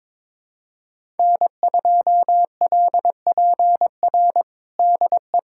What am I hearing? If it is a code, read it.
Morse code, heard as a keyed tone, about 22 wpm, 700 Hz. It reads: N2LPR DE
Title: USA Callsign DE Callsign - 5 Characters - 22wpm